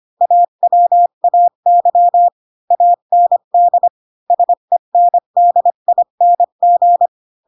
Morse code, heard as a keyed tone, about 25 wpm, 700 Hz. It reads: AWAY AND SENDING
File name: Pinocchio / Pinocchio ch3 20 25 ik1hge